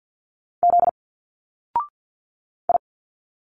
Morse code, heard as a keyed tone, about 51 wpm, 700 Hz. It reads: Z I
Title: Single Letters 50wpm